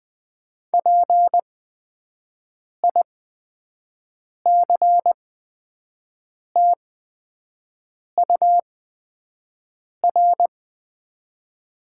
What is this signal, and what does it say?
Morse code, heard as a keyed tone, about 20 wpm, spaced as for 5 wpm, 700 Hz. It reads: PICTUR
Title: 8x Character Spacing - Top 300 Words - 20wpm